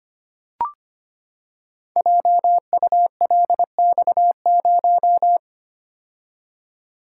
Morse code, heard as a keyed tone, about 25 wpm, 700 Hz. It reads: JULX0
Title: Code Groups - 5 Letter:Numbers - 25wpm